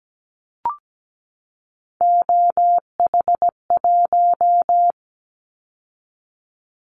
Morse code, heard as a keyed tone, about 17 wpm, 700 Hz. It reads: OH1
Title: Callsign Prefixes for Top 50 DXCC - 17wpm